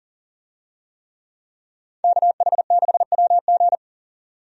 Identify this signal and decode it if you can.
Morse code, heard as a keyed tone, about 40 wpm, 700 Hz. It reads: KH6WG
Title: DXCC-100 Call Signs - 5 Characters - 40wpm